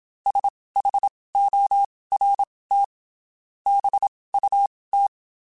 Morse code, heard as a keyed tone, about 27 wpm, spaced as for 19 wpm, 800 Hz. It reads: SHORT BUT
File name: QSOs Spaced